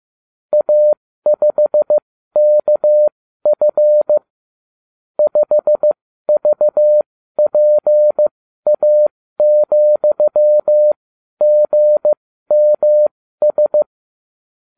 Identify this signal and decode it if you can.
Morse code, heard as a keyed tone, about 15 wpm, spaced as for 12 wpm, 600 Hz. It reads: A5KF 5VPA,GMS